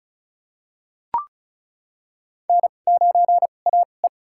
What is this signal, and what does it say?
Morse code, heard as a keyed tone, about 35 wpm, spaced as for 25 wpm, 700 Hz. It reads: N9AE
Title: USA Call Signs - 4 Characters - 2x Character Spacing - 35wpm